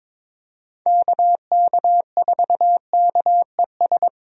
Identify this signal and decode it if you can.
Morse code, heard as a keyed tone, about 22 wpm, 700 Hz. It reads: KK4KES
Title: USA Call Signs - 6 Characters - Repeated 2x - 22wpm